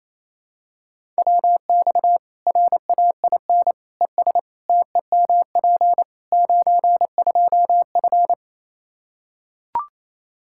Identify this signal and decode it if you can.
Morse code, heard as a keyed tone, about 28 wpm, 700 Hz. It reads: WX RAIN ES TEMP 92F
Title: Practice QSO - Exchange 2 - Info - 28wpm